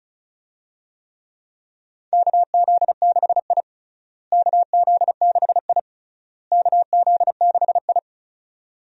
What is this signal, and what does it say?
Morse code, heard as a keyed tone, about 34 wpm, 700 Hz. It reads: KZ6I KZ6I KZ6I
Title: USA Call Signs - 4 Characters - Repeated 3x - 35wpm